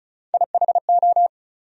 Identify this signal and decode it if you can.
Morse code, heard as a keyed tone, about 34 wpm, 700 Hz. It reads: IHO
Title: USA Call Sign Postfixes - 3 Letters 35wpm